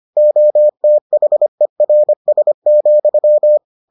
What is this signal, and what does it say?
Morse code, heard as a keyed tone, about 25 wpm, 600 Hz. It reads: OTHERS,